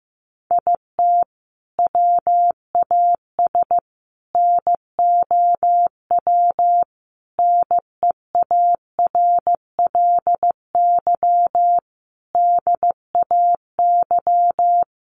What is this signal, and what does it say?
Morse code, heard as a keyed tone, about 15 wpm, 700 Hz. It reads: IT WAS NOW NEARLY DAY
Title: Sentences from Top 1300 Words 15wpm